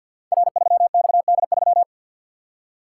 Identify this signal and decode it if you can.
Morse code, heard as a keyed tone, about 53 wpm, 700 Hz. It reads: R3XD3
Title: Code Groups - 5 Letter:Numbers - 50wpm